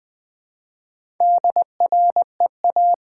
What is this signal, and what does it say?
Morse code, heard as a keyed tone, about 20 wpm, 700 Hz. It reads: DREA
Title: Build A Word - 1 Letter At A Time - 5-Letter Words - 20wpm